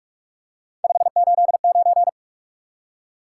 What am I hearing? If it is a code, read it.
Morse code, heard as a keyed tone, about 45 wpm, 700 Hz. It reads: 589
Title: Top 180 QSO Elements 45wpm